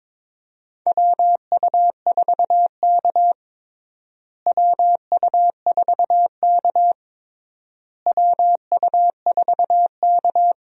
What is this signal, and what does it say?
Morse code, heard as a keyed tone, about 22 wpm, 700 Hz. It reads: WU4K WU4K WU4K
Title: USA Call Signs - 4 Characters - Repeated 3x - 22wpm